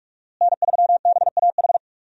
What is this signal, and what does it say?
Morse code, heard as a keyed tone, about 47 wpm, 700 Hz. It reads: N3BAH
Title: USA Call Signs - 5 Characters - Repeated 2x - 45wpm